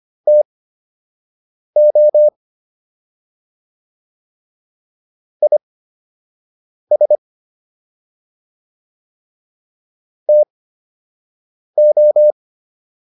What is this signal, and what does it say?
Morse code, heard as a keyed tone, about 25 wpm, spaced as for 6 wpm, 600 Hz. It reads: TO IS TO